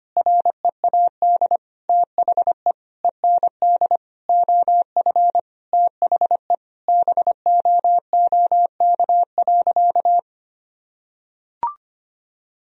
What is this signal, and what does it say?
Morse code, heard as a keyed tone, about 25 wpm, 700 Hz. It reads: READ THE END OF THE BOOK.
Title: Sentences from Top 1000 Words - Review Part 1 - 25wpm